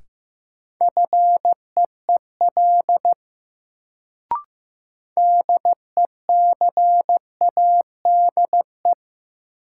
Morse code, heard as a keyed tone, about 15 wpm, 700 Hz. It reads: FEEL DECADE
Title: Sets of 5 Words - New Word Drill - 15wpm